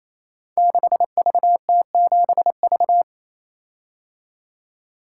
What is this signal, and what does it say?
Morse code, heard as a keyed tone, about 27 wpm, 700 Hz. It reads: 6VT7V
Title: Code Groups - 5 Letter:Numbers - 28wpm